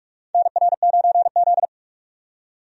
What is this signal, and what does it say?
Morse code, heard as a keyed tone, about 44 wpm, 700 Hz. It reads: NR9Z
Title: USA Call Signs - 4 Characters - 45wpm